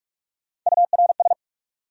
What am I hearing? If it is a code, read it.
Morse code, heard as a keyed tone, about 41 wpm, 700 Hz. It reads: URS